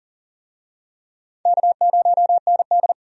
Morse code, heard as a keyed tone, about 40 wpm, 690 Hz. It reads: K0ND
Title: USA Call Signs - 4 Characters - 40wpm